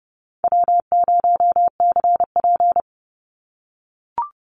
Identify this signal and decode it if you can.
Morse code, heard as a keyed tone, about 30 wpm, 700 Hz. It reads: W0CP
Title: NA SOTA Activator Callsigns 30wpm